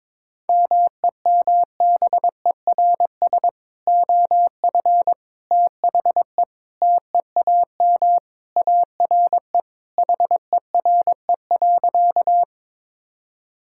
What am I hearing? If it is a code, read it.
Morse code, heard as a keyed tone, about 21 wpm, 700 Hz. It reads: MEMBERS OF THE TEAM ARE HERE.
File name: practice / 22wpm / Sentences with 3-Word Phrases - 22wpm